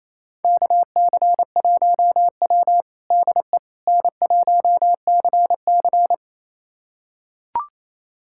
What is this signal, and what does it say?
Morse code, heard as a keyed tone, about 28 wpm, 700 Hz. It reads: KC1W DE N1CC
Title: USA Callsign DE Callsign - 4 Characters - 28wpm